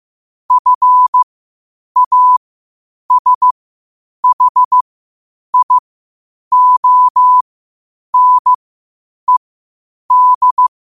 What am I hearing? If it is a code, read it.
Morse code, heard as a keyed tone, about 15 wpm, spaced as for 8 wpm, 1000 Hz. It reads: FASHIONED
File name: qotdfiles / 8wpm